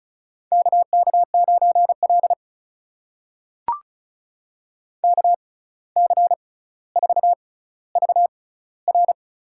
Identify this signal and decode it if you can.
Morse code, heard as a keyed tone, about 36 wpm, spaced as for 17 wpm, 700 Hz. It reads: KK9L KC4VR
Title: USA Call Signs - 6x Character Spacing - 35wpm